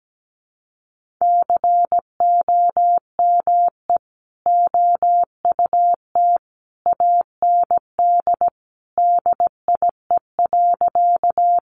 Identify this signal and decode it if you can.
Morse code, heard as a keyed tone, about 17 wpm, 700 Hz. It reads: COME OUT AND DIE.
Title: Sentences from Top 1300 Words 17wpm